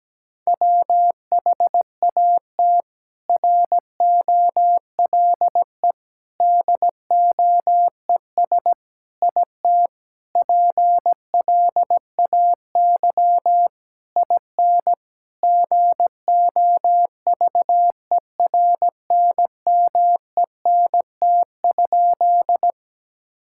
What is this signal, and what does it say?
Morse code, heard as a keyed tone, about 17 wpm, 700 Hz. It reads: WHAT ROLE DOES IT PLAY IN GOVERNMENT?
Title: Sentences from Top 1000 Words - Review Part 6 - 17wpm